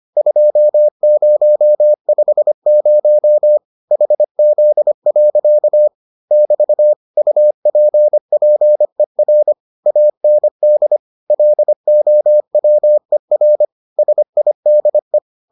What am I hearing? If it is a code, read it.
Morse code, heard as a keyed tone, about 25 wpm, 600 Hz. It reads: 2050 HZ. <BT> UPPER AND LOWER SIDE